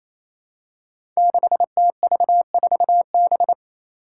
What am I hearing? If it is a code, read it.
Morse code, heard as a keyed tone, about 28 wpm, 700 Hz. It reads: 6TV4B
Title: Five Challenging Letter-Number Combinations 28wpm